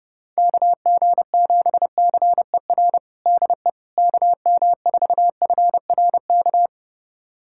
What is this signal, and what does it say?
Morse code, heard as a keyed tone, about 30 wpm, 700 Hz. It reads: KG7CER DE KM4FRK
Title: USA Callsign DE Callsign 30wpm